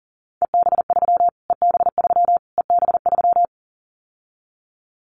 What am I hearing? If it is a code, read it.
Morse code, heard as a keyed tone, about 40 wpm, 700 Hz. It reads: EB3 EB3 EB3
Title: Callsign Prefixes for Top 25 DXCC - Repeated 3x - 40wpm